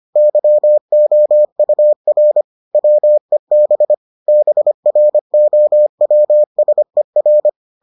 Morse code, heard as a keyed tone, about 25 wpm, 600 Hz. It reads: YOUR WEB BROWSER